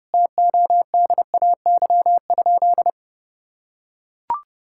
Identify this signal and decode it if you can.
Morse code, heard as a keyed tone, about 32 wpm, 700 Hz. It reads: TODAY?
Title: Sentences from Top 1000 Words - Review Part 5 - 30wpm